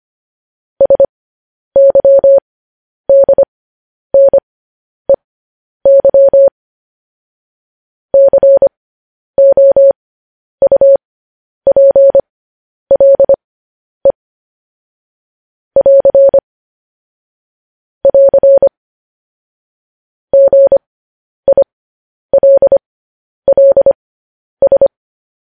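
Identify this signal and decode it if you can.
Morse code, heard as a keyed tone, about 25 wpm, spaced as for 10 wpm, 550 Hz. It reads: SYDNEY COUPLE <AR> <AR> GILLS
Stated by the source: Morse code news headlines broadcast, opening with vvv.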